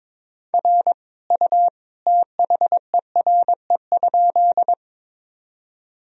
Morse code, heard as a keyed tone, about 22 wpm, 700 Hz. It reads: R U THERE?
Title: Sentences from Top 100 Words and Abbreviations 22wpm